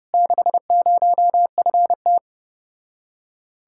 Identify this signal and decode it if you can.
Morse code, heard as a keyed tone, about 33 wpm, 700 Hz. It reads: 60FT